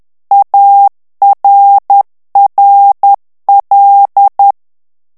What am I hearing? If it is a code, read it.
Morse code, heard as a keyed tone, about 10 wpm, 800 Hz. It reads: ARRL
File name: ANGLAIS texte clair 3 vitesse 10 mots minute